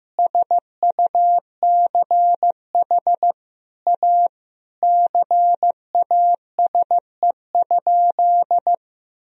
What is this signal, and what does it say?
Morse code, heard as a keyed tone, about 15 wpm, 700 Hz. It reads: SUCH A CASE?
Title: Sentences from Top 1000 Words - Review Part 2 - 15wpm